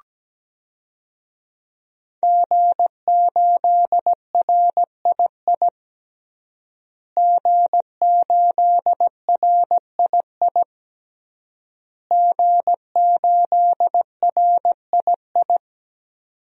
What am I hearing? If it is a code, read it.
Morse code, heard as a keyed tone, about 17 wpm, 700 Hz. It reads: G8RII G8RII G8RII
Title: DXCC-100 Call Signs - 5 Characters - Repeated 3x - 17wpm